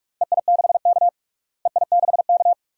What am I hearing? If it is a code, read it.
Morse code, heard as a keyed tone, about 47 wpm, 700 Hz. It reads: EI6K EI6K
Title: DXCC-100 Call Signs - 4 Characters - Repeated 2x - 45wpm